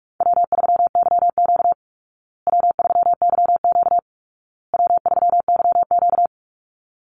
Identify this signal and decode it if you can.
Morse code, heard as a keyed tone, about 44 wpm, 700 Hz. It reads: W3YQ W3YQ W3YQ
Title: Call Signs Repeated 2x - 45wpm